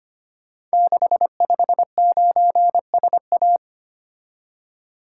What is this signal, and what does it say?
Morse code, heard as a keyed tone, about 25 wpm, 700 Hz. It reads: 659SA
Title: Code Groups - 5 Letter:Numbers - 25wpm